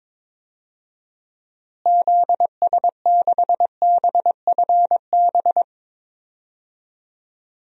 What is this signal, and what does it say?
Morse code, heard as a keyed tone, about 22 wpm, 700 Hz. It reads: ZS6BFB
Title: DXCC-100 Call Signs - 6 Characters - Repeated 2x - 22wpm